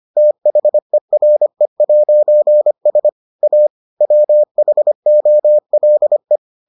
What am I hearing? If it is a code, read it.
Morse code, heard as a keyed tone, about 25 wpm, 600 Hz. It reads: THERE'S A WHOLE